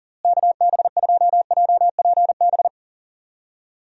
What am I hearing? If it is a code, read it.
Morse code, heard as a keyed tone, about 40 wpm, 700 Hz. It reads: KB2JPB
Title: USA Call Signs - 6 Characters - Repeated 3x - 40wpm